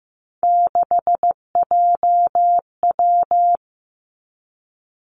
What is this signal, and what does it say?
Morse code, heard as a keyed tone, about 15 wpm, 700 Hz. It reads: 6JW
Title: Three Challenging Letter-Number Combinations 15wpm